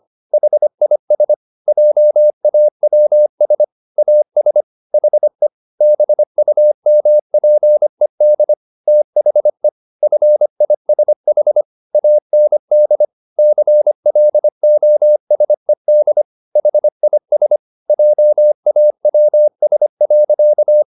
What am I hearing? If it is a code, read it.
Morse code, heard as a keyed tone, about 25 wpm, 600 Hz. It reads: HIS JAWS AS HE BUMPED THE FISH AND CLOSED HIS JAWS.